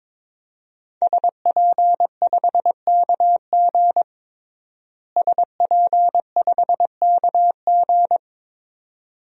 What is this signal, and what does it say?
Morse code, heard as a keyed tone, about 22 wpm, 700 Hz. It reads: SP5KG SP5KG
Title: DXCC-100 Call Signs - 5 Characters - Repeated 2x - 22wpm